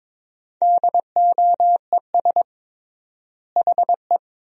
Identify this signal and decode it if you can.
Morse code, heard as a keyed tone, about 22 wpm, 700 Hz. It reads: DOES HE
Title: 3x Word-Spacing - Sentences from Top 300 Words - 22wpm